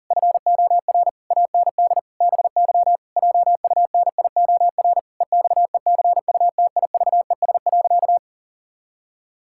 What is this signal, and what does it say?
Morse code, heard as a keyed tone, about 40 wpm, 700 Hz. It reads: FOR AND BY JUNIOR EXECUTIVES.
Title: Pangram Challenge v2 - 40wpm